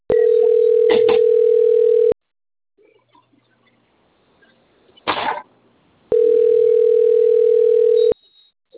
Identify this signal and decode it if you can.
Ringback tone (North America), 440 + 480 Hz.